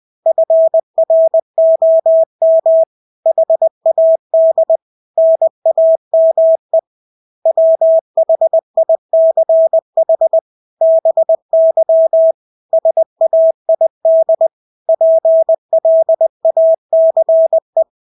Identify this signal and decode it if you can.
Morse code, heard as a keyed tone, about 20 wpm, 650 Hz. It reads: FROM HAD NAME WHICH BY SAID PLACE
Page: Engelske ord 20 wpm